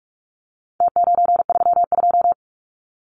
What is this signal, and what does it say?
Morse code, heard as a keyed tone, about 44 wpm, 700 Hz. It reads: T932
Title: Numbers - Contest Style - 4-Digit 45wpm